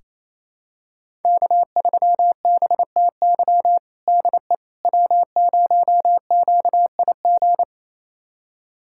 Morse code, heard as a keyed tone, about 28 wpm, 700 Hz. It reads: K3BTY DE W0QIG
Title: USA Callsign DE Callsign - 5 Characters - 28wpm